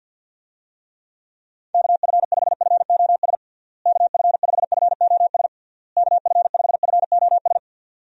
Morse code, heard as a keyed tone, about 51 wpm, 700 Hz. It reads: KF5FOS KF5FOS KF5FOS
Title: USA Call Signs - 6 Characters - Repeated 3x - 50wpm